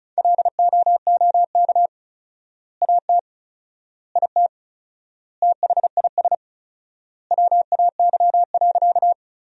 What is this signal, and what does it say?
Morse code, heard as a keyed tone, about 34 wpm, 700 Hz. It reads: LOOK AT IT THIS WAY.
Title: 4x Word-Spacing - Sentences from Top 100 Words - 35wpm